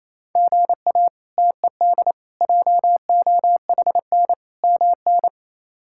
Morse code, heard as a keyed tone, about 27 wpm, 700 Hz. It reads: GA TED JOHN MN
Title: K1USN SST QSO Practice - 28wpm